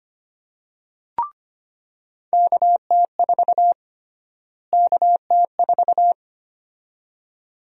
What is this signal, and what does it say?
Morse code, heard as a keyed tone, about 25 wpm, 700 Hz. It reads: KT4 KT4
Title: USA Call Sign Prefixes - Repeated 2x - 25wpm